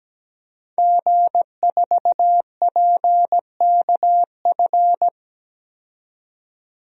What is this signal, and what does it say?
Morse code, heard as a keyed tone, about 17 wpm, 700 Hz. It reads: G4PKF
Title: DXCC-100 Call Signs - 5 Characters - 17wpm